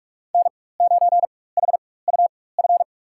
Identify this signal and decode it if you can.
Morse code, heard as a keyed tone, about 46 wpm, spaced as for 21 wpm, 700 Hz. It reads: N9HUF
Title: USA Call Signs - 5 Characters - 4x Character Spacing - 45wpm